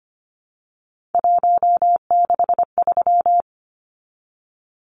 Morse code, heard as a keyed tone, about 25 wpm, 700 Hz. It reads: 163
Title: Numbers - 3-Digit 25wpm